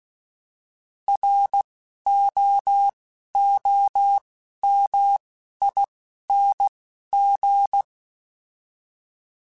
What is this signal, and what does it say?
Morse code, heard as a keyed tone, about 16 wpm, 800 Hz. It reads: R O O M I N G